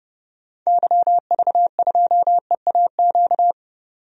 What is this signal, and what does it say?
Morse code, heard as a keyed tone, about 31 wpm, 700 Hz. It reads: YV2EAQ
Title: DXCC-100 Call Signs - 6 Characters - 30wpm